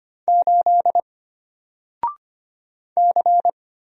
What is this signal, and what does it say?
Morse code, heard as a keyed tone, about 25 wpm, 700 Hz. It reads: 8 C
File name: Single-Letter-Number-25wpm